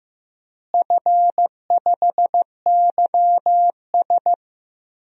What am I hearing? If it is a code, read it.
Morse code, heard as a keyed tone, about 15 wpm, 700 Hz. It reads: F5YS
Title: DXCC-100 Call Signs - 4 Characters - Repeated 2x - 15wpm